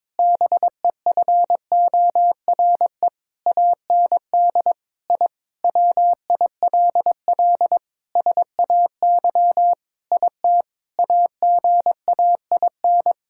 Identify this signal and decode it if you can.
Morse code, heard as a keyed tone, about 21 wpm, 700 Hz. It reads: BEFORE AND I WILL SAY IT AGAIN
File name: practice / 22wpm / Sentences with 8-Word Phrases - 22wpm